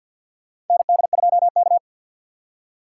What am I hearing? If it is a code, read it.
Morse code, heard as a keyed tone, about 52 wpm, 700 Hz. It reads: ND2K
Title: USA Call Signs - 2x Character Spacing - 50wpm